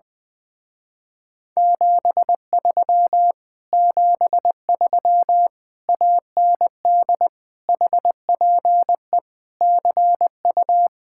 Morse code, heard as a keyed tone, about 20 wpm, 700 Hz. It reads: 73 73 AND HPE CU
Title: Speed-Racing Common QSO Phrases Encore - 20wpm